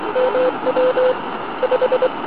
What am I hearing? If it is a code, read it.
Morse code, heard as a keyed tone, about 23 wpm, 510 Hz, with noise. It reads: MW 5